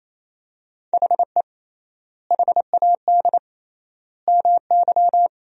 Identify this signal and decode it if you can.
Morse code, heard as a keyed tone, about 28 wpm, 700 Hz. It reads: HE HAD MY
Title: 3x Word-Spacing - Sentences from Top 400 Words - 28wpm